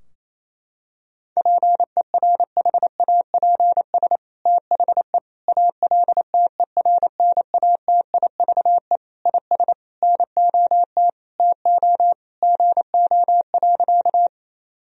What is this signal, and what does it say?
Morse code, heard as a keyed tone, about 28 wpm, 700 Hz. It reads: PERHAPS THE ALTERNATIVE IS NOT TO GO.
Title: Sentences from Top 1000 Words - Review Part 4 - 28wpm